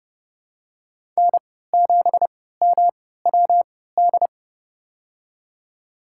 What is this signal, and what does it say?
Morse code, heard as a keyed tone, about 29 wpm, spaced as for 17 wpm, 700 Hz. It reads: N7MWD
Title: USA Call Signs - 5 Characters - 3x Character Spacing - 30wpm